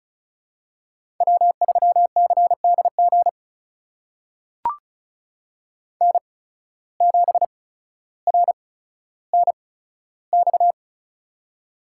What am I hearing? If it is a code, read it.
Morse code, heard as a keyed tone, about 36 wpm, spaced as for 15 wpm, 700 Hz. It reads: W3CDG N7RNX
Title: USA Call Signs - 8x Character Spacing - 35wpm